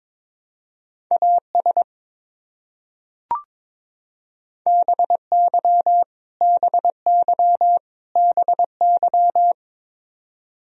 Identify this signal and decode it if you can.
Morse code, heard as a keyed tone, about 22 wpm, 700 Hz. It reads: AS BY BY BY
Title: Top 100 Words Repeated 22wpm